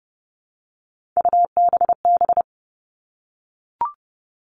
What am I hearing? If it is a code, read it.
Morse code, heard as a keyed tone, about 30 wpm, 700 Hz. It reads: UBB
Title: All Cut Numbers - 3-Digit 30wpm